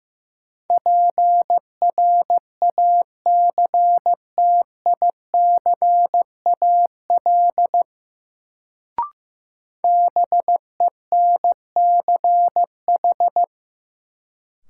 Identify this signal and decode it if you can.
Morse code, heard as a keyed tone, about 15 wpm, 700 Hz. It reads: PRACTICAL BENCH
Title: Sentences from Top 1400 Words - New Word Drill Alt - 15wpm